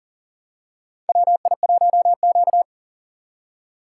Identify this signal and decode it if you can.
Morse code, heard as a keyed tone, about 42 wpm, 700 Hz. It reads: WI1Q
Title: USA Call Signs - 4 Characters - 4x Character Spacing - 40wpm